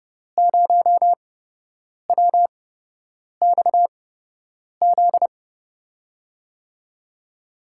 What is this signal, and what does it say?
Morse code, heard as a keyed tone, about 30 wpm, spaced as for 8 wpm, 700 Hz. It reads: 0WXZ
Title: USA Call Signs - 6 Characters - 8x Character Spacing - 30wpm